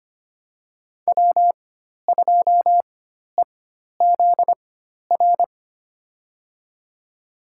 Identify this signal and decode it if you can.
Morse code, heard as a keyed tone, about 25 wpm, spaced as for 12 wpm, 700 Hz. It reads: W2EZR
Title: USA Call Signs - 5 Characters - 4x Character Spacing - 25wpm